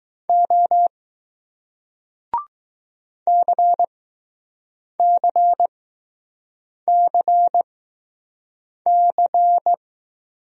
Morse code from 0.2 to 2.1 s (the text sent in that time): O